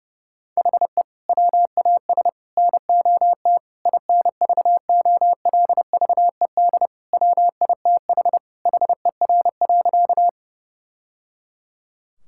Morse code, heard as a keyed tone, about 30 wpm, 700 Hz. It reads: HE WAS NOT INVOLVED WITH HER.
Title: Sentences from Top 1000 Words - Review Part 5 - 30wpm